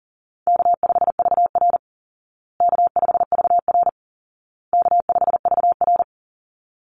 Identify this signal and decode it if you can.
Morse code, heard as a keyed tone, about 40 wpm, 700 Hz. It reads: K5VR K5VR K5VR
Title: Easy Call Signs Repeated 3x - 40wpm